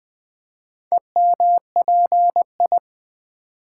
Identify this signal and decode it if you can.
Morse code, heard as a keyed tone, about 21 wpm, 700 Hz. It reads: EMPI
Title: Build A Word - 2 Letters At A Time - 6-Letter Words - 20wpm